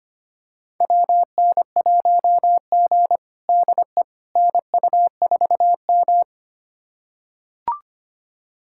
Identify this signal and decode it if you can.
Morse code, heard as a keyed tone, about 25 wpm, 700 Hz. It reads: WN1G DE NU4M
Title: USA Callsign DE Callsign - 4 Characters - 25wpm